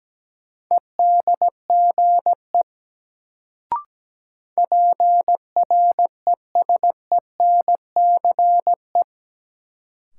Morse code, heard as a keyed tone, about 17 wpm, 700 Hz. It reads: EDGE PRESENCE
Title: Sets of 3 Words - New Word Drill - 17wpm